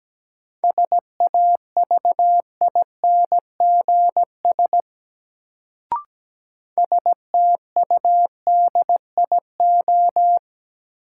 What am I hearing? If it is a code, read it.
Morse code, heard as a keyed tone, about 17 wpm, 700 Hz. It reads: SAVINGS STUDIO
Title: Sentences from Top 1200 Words - New Word Drill - 17wpm